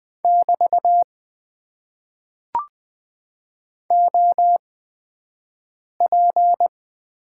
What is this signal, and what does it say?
Morse code, heard as a keyed tone, about 20 wpm, spaced as for 5 wpm, 700 Hz. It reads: <BT> OP
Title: Top 90 QSO Elements - 8x Character Spacing - 20wpm